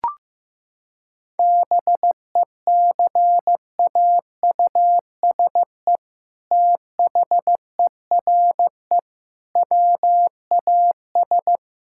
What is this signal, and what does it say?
Morse code, heard as a keyed tone, about 16 wpm, 700 Hz. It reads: BECAUSE THERE WAS
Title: Sentences from Top 1600 Words 15wpm